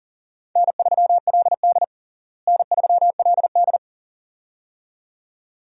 Morse code, heard as a keyed tone, about 37 wpm, 700 Hz. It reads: N3LD N3LD
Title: USA Call Signs - 4 Characters - Repeated 2x - 40wpm